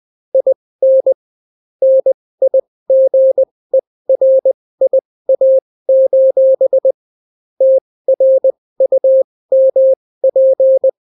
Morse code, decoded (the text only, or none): IN NIGERIA: TRUMP